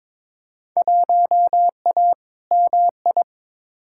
Morse code, heard as a keyed tone, about 22 wpm, 700 Hz. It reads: 1A MI
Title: Field Day Exchanges - 22wpm